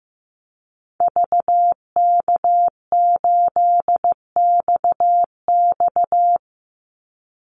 Morse code, heard as a keyed tone, about 15 wpm, 700 Hz. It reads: VK8XX
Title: Call Signs Repeated 2x - 15wpm